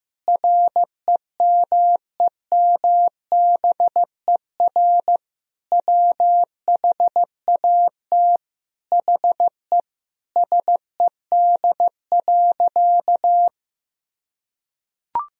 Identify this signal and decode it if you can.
Morse code, heard as a keyed tone, about 15 wpm, 700 Hz. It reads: REMEMBER WHAT HE SED.
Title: Sentences from Top 400 Words and Abbreviations 15wpm